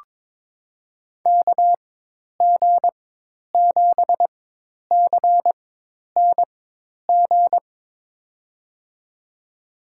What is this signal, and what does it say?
Morse code, heard as a keyed tone, about 23 wpm, spaced as for 10 wpm, 700 Hz. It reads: KG7CNG
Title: USA Call Signs - 6 Characters - 4x Character Spacing - 22wpm